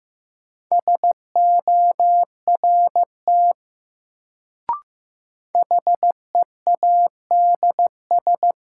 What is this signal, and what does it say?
Morse code, heard as a keyed tone, about 15 wpm, 700 Hz. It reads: SORT HEADS
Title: Sentences from Top 1000 Words - Review Part 3 - New Word Drill - 15wpm